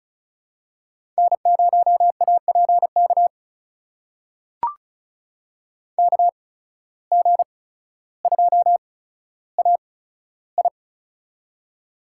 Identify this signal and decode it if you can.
Morse code, heard as a keyed tone, about 36 wpm, spaced as for 15 wpm, 690 Hz. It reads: N0APK KG2AI
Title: USA Call Signs - 5 Characters - 8x Character Spacing - 35wpm